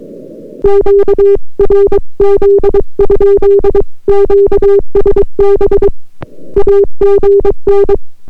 Morse code, heard as a keyed tone, about 22 wpm, 400 Hz. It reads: QRZ? QSB AGN